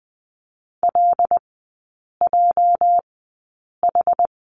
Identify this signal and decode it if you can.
Morse code, heard as a keyed tone, about 20 wpm, spaced as for 8 wpm, 700 Hz. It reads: LJH
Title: 205 Lesson 33e - Letter j - Three Characters 20wpm